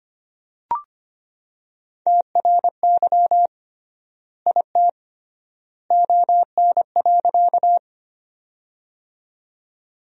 Morse code, heard as a keyed tone, about 25 wpm, 700 Hz. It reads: TRY IT ON.
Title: 3x Word-Spacing - Sentences from Top 200 Words - 25wpm